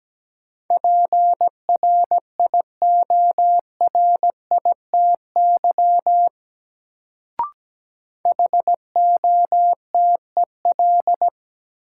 Morse code, heard as a keyed tone, about 17 wpm, 700 Hz. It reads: PRIORITY HOTEL
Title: Sentences from Top 1600 Words - New Word Drill - 17wpm